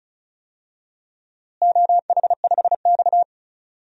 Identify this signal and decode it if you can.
Morse code, heard as a keyed tone, about 36 wpm, 700 Hz. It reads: OH5X
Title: DXCC-100 Call Signs - 4 Characters - Repeated 2x - 35wpm